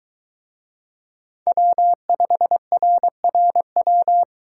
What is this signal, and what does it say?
Morse code, heard as a keyed tone, about 23 wpm, 700 Hz. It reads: W5RRW